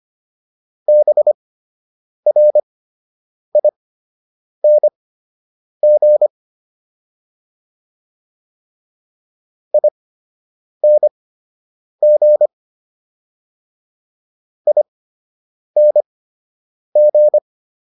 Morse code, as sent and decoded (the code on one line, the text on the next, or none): -... .-. .. -. --. / .. -. --. / .. -. --.
BRING ING ING